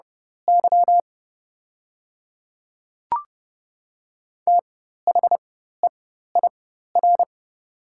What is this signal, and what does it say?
Morse code, heard as a keyed tone, about 30 wpm, spaced as for 14 wpm, 700 Hz. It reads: Y THEIR
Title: 4x Character Spacing - Top 100 Words - 30wpm